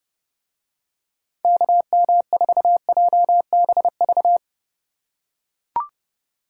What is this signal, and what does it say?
Morse code, heard as a keyed tone, about 29 wpm, 700 Hz. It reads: KM4JBV